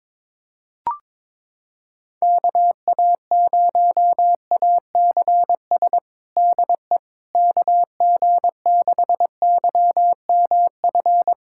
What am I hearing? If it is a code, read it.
Morse code, heard as a keyed tone, about 22 wpm, 700 Hz. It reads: KA0ACS DE KG6YMF
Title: USA Callsign DE Callsign 22wpm